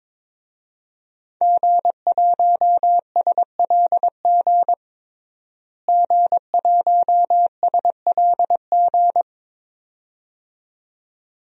Morse code, heard as a keyed tone, about 22 wpm, 700 Hz. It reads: G1SLG G1SLG
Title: DXCC-100 Call Signs - 5 Characters - Repeated 2x - 22wpm